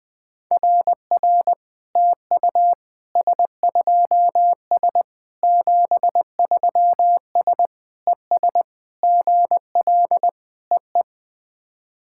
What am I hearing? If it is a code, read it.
Morse code, heard as a keyed tone, about 20 wpm, 700 Hz. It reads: RR TU S2S 73S ES GL EE
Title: SOTA QSO Practice 20wpm